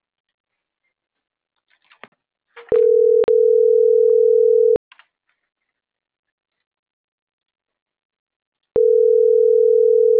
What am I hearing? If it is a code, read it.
Ringback tone (North America), 440 + 480 Hz.